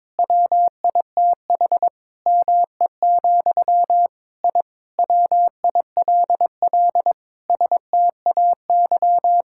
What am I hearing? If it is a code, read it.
Morse code, heard as a keyed tone, about 22 wpm, 700 Hz. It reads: WITH ME, I WILL STAY
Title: Sentences from Top 1000 Words - Review Part 6 - 22wpm